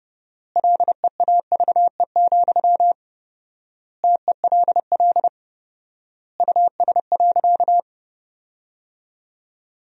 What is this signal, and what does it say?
Morse code, heard as a keyed tone, about 32 wpm, 700 Hz. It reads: LEAVE, TELL US.
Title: 4x Word-Spacing - Sentences from Top 400 Words - 30wpm